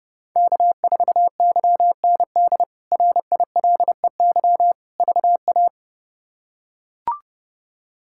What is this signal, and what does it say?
Morse code, heard as a keyed tone, about 31 wpm, 700 Hz. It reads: K4YND RILEY VA
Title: K1USN SST QSO Practice - 30wpm